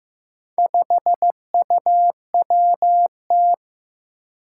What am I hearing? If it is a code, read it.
Morse code, heard as a keyed tone, about 15 wpm, 700 Hz. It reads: 5UWT
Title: Four Challenging Letter-Number Combinations 15wpm